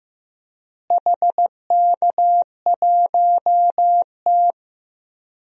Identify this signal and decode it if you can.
Morse code, heard as a keyed tone, about 15 wpm, 700 Hz. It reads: HK1T
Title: DXCC-100 Call Signs - 4 Characters - Repeated 3x - 15wpm